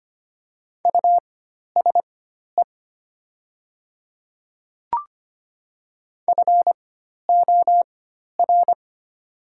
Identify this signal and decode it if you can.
Morse code, heard as a keyed tone, about 25 wpm, spaced as for 12 wpm, 700 Hz. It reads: USE FOR
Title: 4x Character Spacing - Top 100 Words - 25wpm